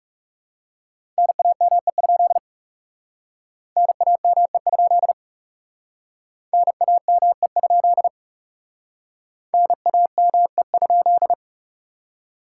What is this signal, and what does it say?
Morse code, heard as a keyed tone, about 38 wpm, 700 Hz. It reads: NAME? NAME? NAME? NAME?